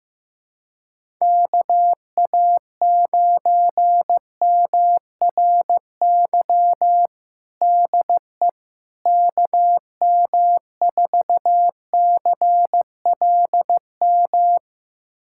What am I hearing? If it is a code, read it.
Morse code, heard as a keyed tone, about 16 wpm, 700 Hz. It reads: KA9MRY DE KM4CLM
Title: USA Callsign DE Callsign - 6 Characters - 15wpm